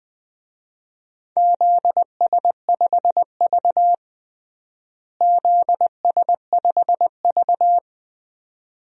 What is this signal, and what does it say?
Morse code, heard as a keyed tone, about 20 wpm, 700 Hz. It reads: ZS5V ZS5V
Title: DXCC-100 Call Signs - 4 Characters - Repeated 2x - 20wpm